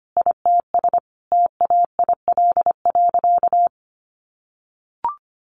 Morse code, heard as a keyed tone, about 25 wpm, 700 Hz. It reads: ITS TAIL.
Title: Easy Phrases 25wpm